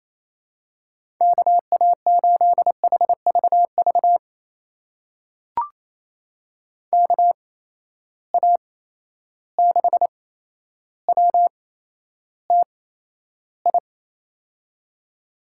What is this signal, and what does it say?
Morse code, heard as a keyed tone, about 28 wpm, spaced as for 12 wpm, 700 Hz. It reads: KA8HVV KA6WTI
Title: USA Call Signs - 6 Characters - 8x Character Spacing - 28wpm